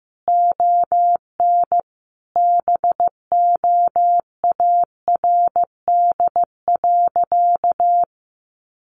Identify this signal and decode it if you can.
Morse code, heard as a keyed tone, about 15 wpm, 700 Hz. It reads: ON BOARD.
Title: Sentences from Top 1400 Words 15wpm